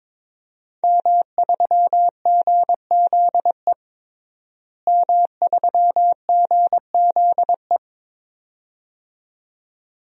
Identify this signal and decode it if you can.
Morse code, heard as a keyed tone, about 22 wpm, 700 Hz. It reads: M3GZE M3GZE
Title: DXCC-100 Call Signs - 5 Characters - Repeated 2x - 22wpm